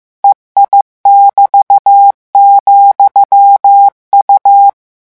Morse code, heard as a keyed tone, about 15 wpm, 800 Hz. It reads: EI<BT>,U
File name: morse mp3 / revision13-vitesse 14 mots